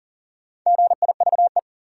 Morse code, heard as a keyed tone, about 40 wpm, 700 Hz. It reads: GIVE
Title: Sentences from Top 200 Words - New Word Drill - 40wpm